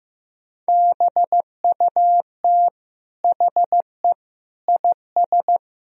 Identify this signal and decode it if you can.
Morse code, heard as a keyed tone, about 15 wpm, 700 Hz. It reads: BUT HE IS
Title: Sentences from Top 1000 Words - Review Part 3 - 15wpm